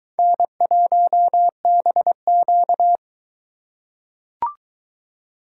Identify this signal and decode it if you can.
Morse code, heard as a keyed tone, about 22 wpm, 700 Hz. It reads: N1BQ